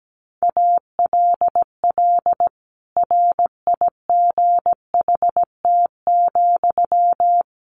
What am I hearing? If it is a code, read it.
Morse code, heard as a keyed tone, about 17 wpm, 700 Hz. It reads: ALL RIGHT,
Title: Sentences from Top 200 Words 17wpm